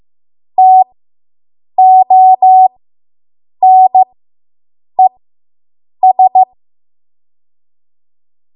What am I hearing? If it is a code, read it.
Morse code, heard as a keyed tone, about 15 wpm, spaced as for 7 wpm, 750 Hz. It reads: TONES